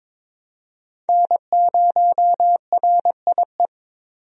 Morse code, heard as a keyed tone, about 22 wpm, 700 Hz. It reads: N0RIE
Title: USA Call Signs - 5 Characters - 22wpm